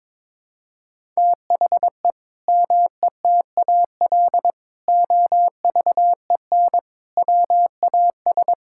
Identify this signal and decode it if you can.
Morse code, heard as a keyed tone, about 22 wpm, 700 Hz. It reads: THE METAL OVEN WAS
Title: Sentences from Top 1000 Words - Review Part 6 - 22wpm